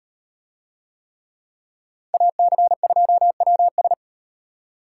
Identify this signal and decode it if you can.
Morse code, heard as a keyed tone, about 38 wpm, 700 Hz. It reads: AC2WS